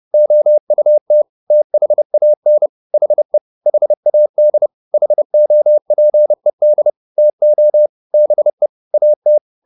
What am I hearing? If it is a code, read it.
Morse code, heard as a keyed tone, about 30 wpm, 600 Hz. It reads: OUT THAN HE HAD HOPED TO BE AT
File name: the old man and the sea / the old man and the sea 1 30wpm